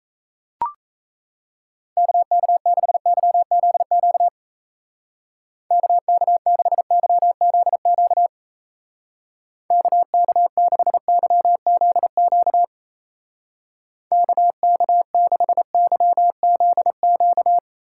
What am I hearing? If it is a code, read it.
Morse code, heard as a keyed tone, about 34 wpm, 700 Hz. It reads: KK6YZQ KK6YZQ KK6YZQ KK6YZQ
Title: Speed Racing - USA 2x3 Call Signs 28wpm